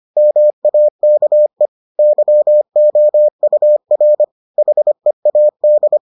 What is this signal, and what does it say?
Morse code, heard as a keyed tone, about 25 wpm, 600 Hz. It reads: MAKE YOUR HEAD